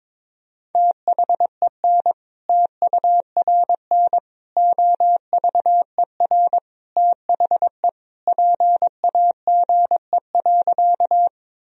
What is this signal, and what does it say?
Morse code, heard as a keyed tone, about 22 wpm, 700 Hz. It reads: THEN TURN OVER THE PAGE.
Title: Sentences from Top 1000 Words - Review Part 1 - 22wpm